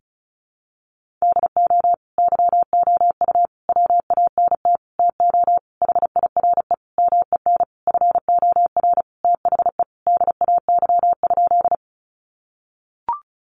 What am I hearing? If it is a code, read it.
Morse code, heard as a keyed tone, about 34 wpm, 700 Hz. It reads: DO YOU WANT TO HIRE MEN FOR THE DAY?
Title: Sentences from Top 1400 Words 35wpm